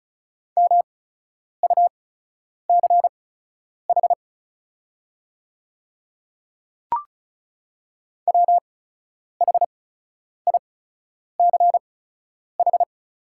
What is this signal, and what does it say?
Morse code, heard as a keyed tone, about 36 wpm, spaced as for 10 wpm, 700 Hz. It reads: MUCH WHICH
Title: 8x Character Spacing - Top 200 Words - 35wpm